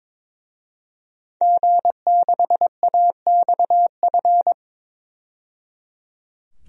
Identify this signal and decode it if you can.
Morse code, heard as a keyed tone, about 22 wpm, 700 Hz. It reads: G6AXF
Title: DXCC-100 Call Signs - 5 Characters - 22wpm